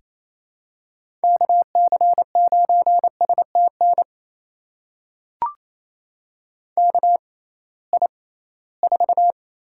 Morse code, heard as a keyed tone, about 28 wpm, spaced as for 18 wpm, 700 Hz. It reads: KC9STN KI4
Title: USA Call Signs - 6 Characters - 6x Character Spacing - 28wpm